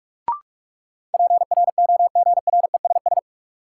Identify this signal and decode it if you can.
Morse code, heard as a keyed tone, about 43 wpm, 700 Hz. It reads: PROGRESS
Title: Sentences from Top 1400 Words - New Word Drill - 45wpm